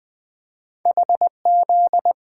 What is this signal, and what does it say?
Morse code, heard as a keyed tone, about 20 wpm, 700 Hz. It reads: HZ
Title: Two Letter-Number Combinations - 20wpm